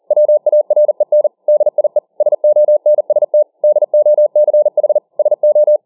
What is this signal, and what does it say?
Morse code, heard as a keyed tone, about 40 wpm, 600 Hz. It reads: WAREN DIE SONST DOCH SO